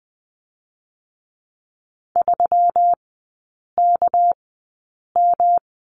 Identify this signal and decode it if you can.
Morse code, heard as a keyed tone, about 20 wpm, spaced as for 8 wpm, 690 Hz. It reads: 3KM
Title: 198 Lesson 32e - Letter k - Three Characters 20wpm